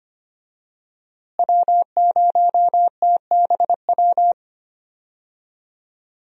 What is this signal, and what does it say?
Morse code, heard as a keyed tone, about 25 wpm, 700 Hz. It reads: W0TBW
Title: USA Call Signs - 5 Characters - Repeated 3x - 25wpm